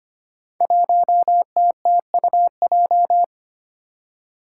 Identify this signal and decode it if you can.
Morse code, heard as a keyed tone, about 25 wpm, 700 Hz. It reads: 1TTUJ
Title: Five Challenging Letter-Number Combinations 25wpm